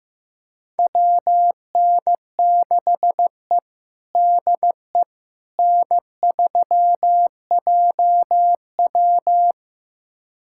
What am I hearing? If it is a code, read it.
Morse code, heard as a keyed tone, about 15 wpm, 700 Hz. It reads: WN6E DE N3JW
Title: USA Callsign DE Callsign - 4 Characters - 15wpm